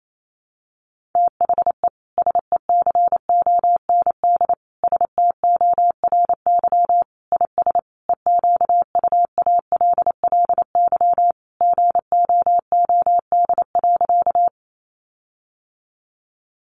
Morse code, heard as a keyed tone, about 28 wpm, 700 Hz. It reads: THE SECOND STORY IS EQUALLY GOOD.
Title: Sentences from Top 1400 Words 28wpm